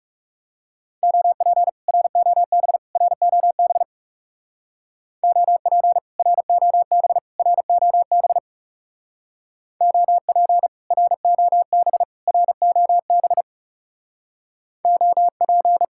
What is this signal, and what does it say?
Morse code, heard as a keyed tone, about 37 wpm, 700 Hz. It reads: OP ROB ROB OP ROB ROB OP ROB ROB OP
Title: Speed-Racing Common QSO Phrases - 30wpm